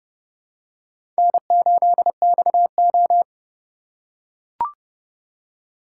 Morse code, heard as a keyed tone, about 28 wpm, 700 Hz. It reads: N8XO